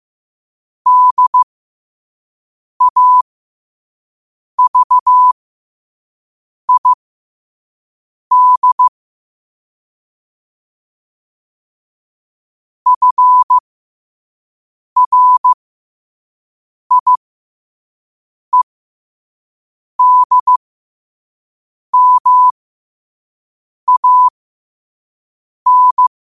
Morse code, heard as a keyed tone, about 15 wpm, spaced as for 5 wpm, 1000 Hz. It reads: DAVID FRIEDMAN
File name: qotdfiles / 5wpm